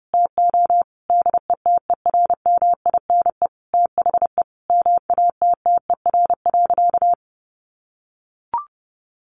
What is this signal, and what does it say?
Morse code, heard as a keyed tone, about 30 wpm, 700 Hz. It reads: TO DETERMINE THE MATTER.
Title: Sentences from Top 1300 Words 30wpm